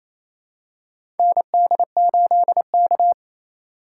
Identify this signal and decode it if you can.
Morse code, heard as a keyed tone, about 28 wpm, 700 Hz. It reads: ND8K
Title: USA Call Signs - 4x Character Spacing - 28wpm